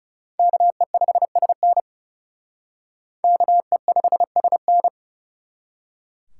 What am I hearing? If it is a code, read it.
Morse code, heard as a keyed tone, about 34 wpm, 700 Hz. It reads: KE5SN KE5SN
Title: Speed Racing - USA Call Signs 30wpm